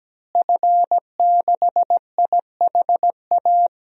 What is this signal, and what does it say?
Morse code, heard as a keyed tone, about 17 wpm, 700 Hz. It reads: F6IHA
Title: DXCC-100 Call Signs - 5 Characters - Repeated 3x - 17wpm